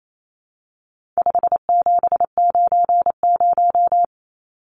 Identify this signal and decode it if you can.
Morse code, heard as a keyed tone, about 28 wpm, 700 Hz. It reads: 5790
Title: Numbers - 4-Digit 28wpm